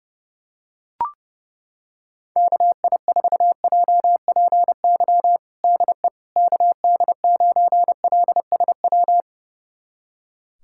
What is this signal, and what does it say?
Morse code, heard as a keyed tone, about 30 wpm, 700 Hz. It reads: KI4JPY DE KD9LSW
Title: USA Callsign DE Callsign 30wpm